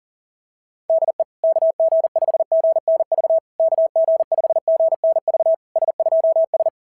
Morse code, heard as a keyed tone, about 41 wpm, 690 Hz. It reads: DE KG5GNV KG5GNV S2S
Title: SOTA QSO Practice 40wpm